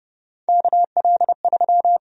Morse code, heard as a keyed tone, about 30 wpm, 700 Hz. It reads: KL3
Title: USA Call Sign Prefixes - 30wpm